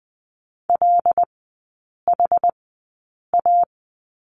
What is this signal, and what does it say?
Morse code, heard as a keyed tone, about 20 wpm, spaced as for 8 wpm, 700 Hz. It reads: LHA
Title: 074 Lesson 13e - Letter L - Three Characters 20wpm